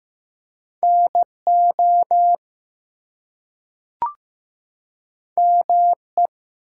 Morse code, heard as a keyed tone, about 15 wpm, 700 Hz. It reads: NO ME
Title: Two Letter Words - 15wpm